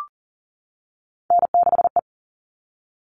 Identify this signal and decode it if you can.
Morse code, heard as a keyed tone, about 40 wpm, 700 Hz. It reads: N6E
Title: All Cut Numbers - 3-Digit 40wpm